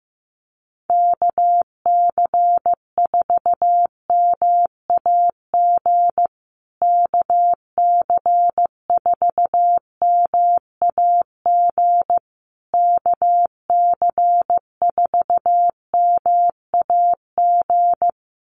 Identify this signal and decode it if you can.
Morse code, heard as a keyed tone, about 15 wpm, 700 Hz. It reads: KC4MAG KC4MAG KC4MAG
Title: USA Call Signs Repeated 3x 15wpm